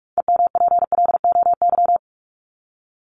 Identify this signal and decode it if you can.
Morse code, heard as a keyed tone, about 44 wpm, 700 Hz. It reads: EMPLOY
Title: Top 1751-2000 Words 45wpm